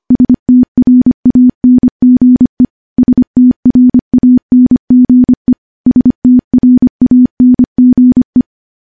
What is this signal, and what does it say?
Morse code, heard as a keyed tone, about 25 wpm, 250 Hz. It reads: STRANGE STRANGE STRANGE
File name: morse / morse-training-25WPM